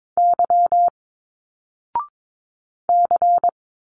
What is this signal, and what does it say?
Morse code, heard as a keyed tone, about 22 wpm, 700 Hz. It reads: Y C
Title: Single Letters 22wpm